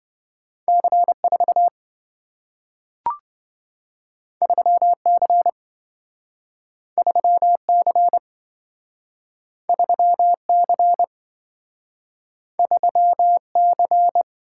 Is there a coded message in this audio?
Yes: Morse code from 0.7 to 14.2 s, heard as a keyed tone, about 27 wpm, 700 Hz. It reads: C4 3C 3C 3C 3C